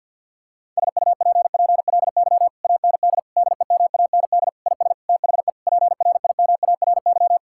Morse code, heard as a keyed tone, about 52 wpm, 700 Hz. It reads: SUPPLY AND DEMAND IS THE PRIMARY
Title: Sentences from Top 1000 Words - Review Part 6 - 50wpm